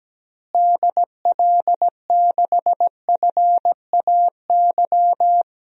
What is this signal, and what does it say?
Morse code, heard as a keyed tone, about 17 wpm, 700 Hz. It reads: DL6FAY
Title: DXCC-100 Call Signs - 6 Characters - Repeated 3x - 17wpm